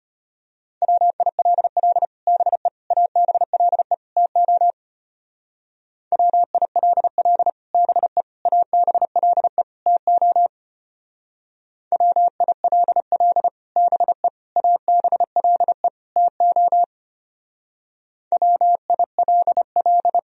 Morse code, heard as a keyed tone, about 32 wpm, 700 Hz. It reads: WILL BE ABLE TO WILL BE ABLE TO WILL BE ABLE TO WILL